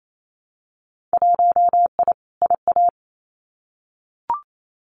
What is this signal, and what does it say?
Morse code, heard as a keyed tone, about 28 wpm, 700 Hz. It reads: 1I IA
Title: Winter Field Day Exchanges 28wpm